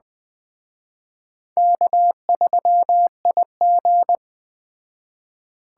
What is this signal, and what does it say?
Morse code, heard as a keyed tone, about 21 wpm, 700 Hz. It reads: K3IG
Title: Speed Racing - USA 1x2 Call Signs 17wpm